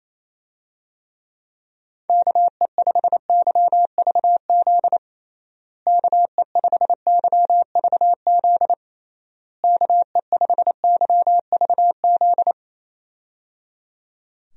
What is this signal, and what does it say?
Morse code, heard as a keyed tone, about 28 wpm, 700 Hz. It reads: KE5YVZ KE5YVZ KE5YVZ
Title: USA Call Signs - 6 Characters - Repeated 3x - 28wpm